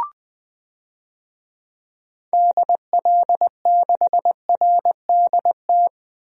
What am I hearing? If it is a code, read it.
Morse code, heard as a keyed tone, about 20 wpm, 700 Hz. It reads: DL6RDT
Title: DXCC-100 Call Signs - 6 Characters - 20wpm